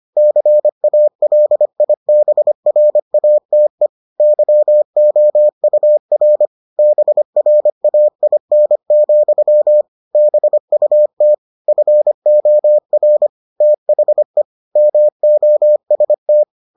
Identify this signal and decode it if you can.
Morse code, heard as a keyed tone, about 25 wpm, 600 Hz. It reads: CALIBRATE YOUR BRAIN, BUT FOR THE MOST